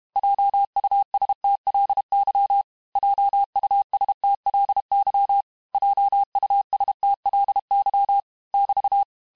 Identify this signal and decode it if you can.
Morse code, heard as a keyed tone, about 32 wpm, 800 Hz. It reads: JUSTLY JUSTLY JUSTLY <BT>
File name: cours-du-mercredi / audios / cours-868-teste-MOTS-vitesse-30mpm